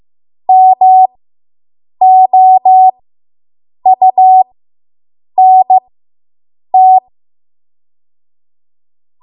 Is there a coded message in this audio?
Yes: Morse code from 0.5 to 7.0 s, heard as a keyed tone, about 15 wpm, spaced as for 7 wpm, 750 Hz. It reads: MOUNT